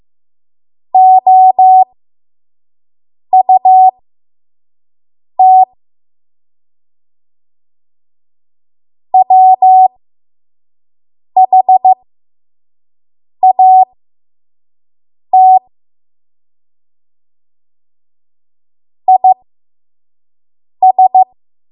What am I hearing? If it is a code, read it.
Morse code, heard as a keyed tone, about 15 wpm, spaced as for 5 wpm, 750 Hz. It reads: OUT WHAT IS
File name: Morse / 05WPM